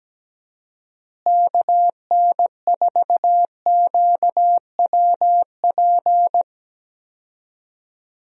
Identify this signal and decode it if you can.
Morse code, heard as a keyed tone, about 17 wpm, 700 Hz. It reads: KN4QWP
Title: USA Call Signs - 6 Characters - Repeated 3x - 17wpm